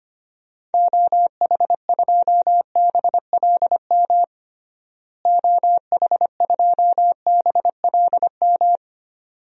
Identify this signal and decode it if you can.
Morse code, heard as a keyed tone, about 25 wpm, 700 Hz. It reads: OH2BLM OH2BLM
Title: DXCC-100 Call Signs - 6 Characters - Repeated 2x - 25wpm